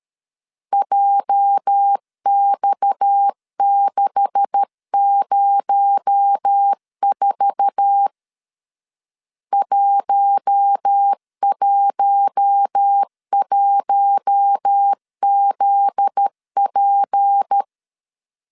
Morse code, heard as a keyed tone, about 13 wpm, 800 Hz. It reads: JX604 111ZP
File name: CW / DET / 1 9 1500C S B 12wpm 3819